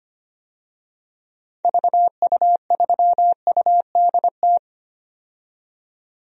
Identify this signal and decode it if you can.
Morse code, heard as a keyed tone, about 25 wpm, 690 Hz. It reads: VU3UDT
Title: DXCC-100 Call Signs - 6 Characters - 25wpm